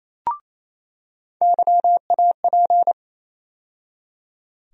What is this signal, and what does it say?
Morse code, heard as a keyed tone, about 28 wpm, 700 Hz. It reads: YAP
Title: USA Call Sign Postfixes - 3 Letters 28wpm